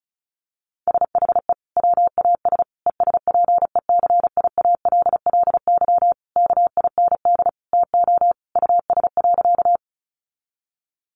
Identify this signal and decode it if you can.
Morse code, heard as a keyed tone, about 34 wpm, 700 Hz. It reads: SHE WAS ESPECIALLY KIND TO US.
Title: Sentences from Top 700 Words 35wpm